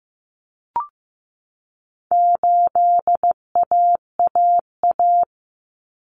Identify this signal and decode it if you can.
Morse code, heard as a keyed tone, about 15 wpm, 700 Hz. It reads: 8AAA
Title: Common Cut Numbers - 4-Digit 15wpm